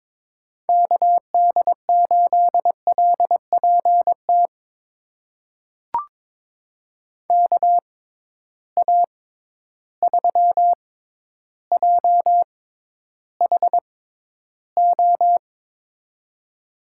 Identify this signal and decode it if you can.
Morse code, heard as a keyed tone, about 22 wpm, spaced as for 11 wpm, 700 Hz. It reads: KD8LPT KA3JHO
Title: USA Call Signs - 6 Characters - 6x Character Spacing - 22wpm